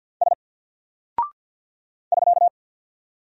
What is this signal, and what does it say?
Morse code, heard as a keyed tone, about 48 wpm, 700 Hz. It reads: S <SK>
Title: Single Character (Letter:Number:Punctuation:Prosign) 50wpm